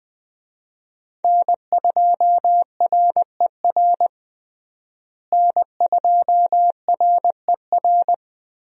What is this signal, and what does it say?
Morse code, heard as a keyed tone, about 20 wpm, 700 Hz. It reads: N2RER N2RER
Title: USA Call Signs - 5 Characters - Repeated 2x - 20wpm